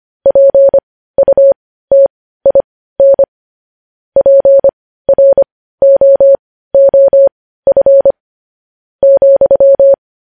Morse code, heard as a keyed tone, about 25 wpm, spaced as for 15 wpm, 550 Hz. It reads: PUTIN PROOF ,
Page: news-headlines-in-morse-code-at-15-wpm